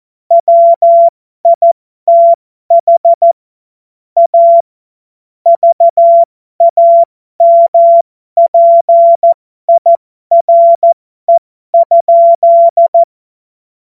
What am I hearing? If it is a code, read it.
Morse code, heard as a keyed tone, about 14 wpm, spaced as for 12 wpm, 680 Hz. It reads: WITH A VAMPIRE?
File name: Morsekurs / Witze